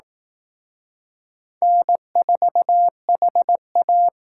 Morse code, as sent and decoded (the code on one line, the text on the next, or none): -. ....- .... .-
N4HA